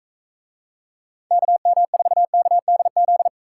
Morse code, heard as a keyed tone, about 44 wpm, 700 Hz. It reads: KM4KDZ